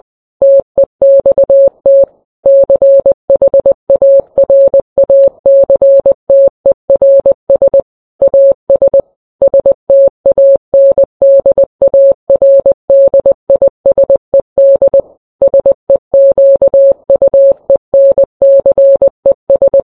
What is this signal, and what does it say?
Morse code, heard as a keyed tone, about 20 wpm, 550 Hz. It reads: TEXT CHARACTERS AS STANDARDISED SEQUENCES